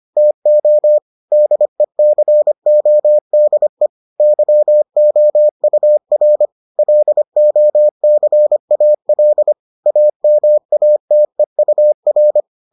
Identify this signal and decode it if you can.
Morse code, heard as a keyed tone, about 25 wpm, 600 Hz. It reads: TO DECODE YOUR LOCAL AMATEUR